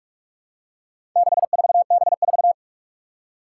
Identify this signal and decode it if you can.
Morse code, heard as a keyed tone, about 46 wpm, 700 Hz. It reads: 64B4
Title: Four Challenging Letter-Number Combinations 45wpm